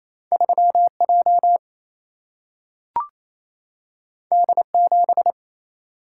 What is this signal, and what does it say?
Morse code, heard as a keyed tone, about 28 wpm, 700 Hz. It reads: 3J D7
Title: Single-Letter Single-Number 28wpm